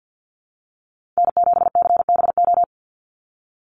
Morse code, heard as a keyed tone, about 50 wpm, 700 Hz. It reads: N7CBO
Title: USA Call Signs 50wpm